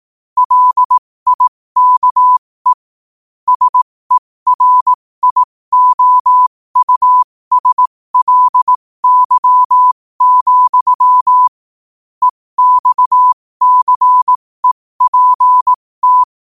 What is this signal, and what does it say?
Morse code, heard as a keyed tone, about 18 wpm, spaced as for 15 wpm, 1000 Hz. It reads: LIKE SERIOUSLY, EXCEPT
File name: qotdfiles / 15wpm